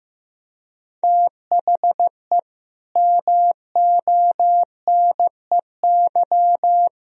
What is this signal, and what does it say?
Morse code, heard as a keyed tone, about 15 wpm, 700 Hz. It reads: THE MONEY
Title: Sentences from Top 1000 Words - Review Part 4 - 15wpm